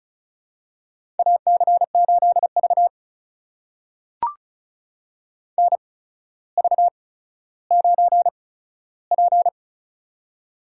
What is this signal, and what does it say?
Morse code, heard as a keyed tone, about 34 wpm, spaced as for 15 wpm, 700 Hz. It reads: AC8V NV9P
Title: USA Call Signs - 4 Characters - 8x Character Spacing - 35wpm